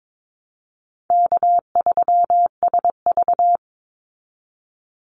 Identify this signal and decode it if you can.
Morse code, heard as a keyed tone, about 22 wpm, 700 Hz. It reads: K3SV
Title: Easy Call Signs Repeated 2x - 22wpm